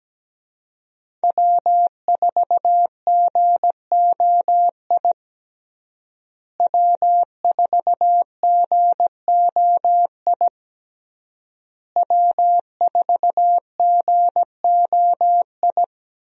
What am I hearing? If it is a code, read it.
Morse code, heard as a keyed tone, about 17 wpm, 700 Hz. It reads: W4GOI W4GOI W4GOI
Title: USA Call Signs - 5 Characters - Repeated 3x - 17wpm